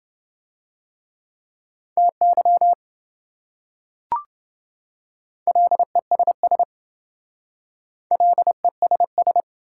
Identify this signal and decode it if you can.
Morse code, heard as a keyed tone, about 27 wpm, 700 Hz. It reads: TY LESS LESS